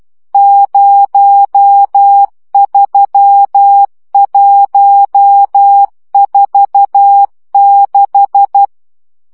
Morse code, heard as a keyed tone, about 12 wpm, 800 Hz. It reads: 03146
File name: morze vizsga 2011-3 10